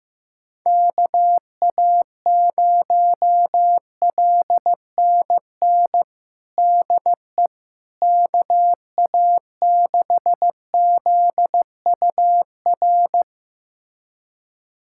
Morse code, heard as a keyed tone, about 15 wpm, 700 Hz. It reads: KA0LNN DE KA6ZUR
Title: USA Callsign DE Callsign - 6 Characters - 15wpm